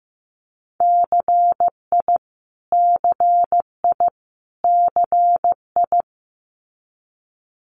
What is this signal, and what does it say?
Morse code, heard as a keyed tone, about 15 wpm, 700 Hz. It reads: CI CI CI
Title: Multi-letter Phonemes Repeated 3x 15wpm